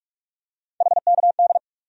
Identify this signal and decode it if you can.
Morse code, heard as a keyed tone, about 42 wpm, 700 Hz. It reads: HKD
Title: USA Call Sign Postfixes - 3 Letters 45wpm